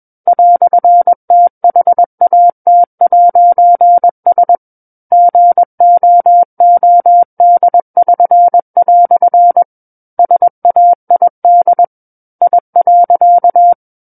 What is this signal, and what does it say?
Morse code, heard as a keyed tone, about 21 wpm, 700 Hz. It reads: "THAT'S GOOD<SN>" SAID I.
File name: Book One / 12-Chapter 12 - 21 WPM